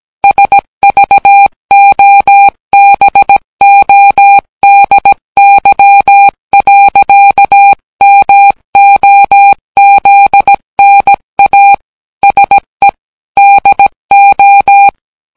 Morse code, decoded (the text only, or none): SVOBODY. MOZNA SE DO